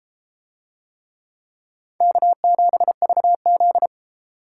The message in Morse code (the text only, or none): K7VZ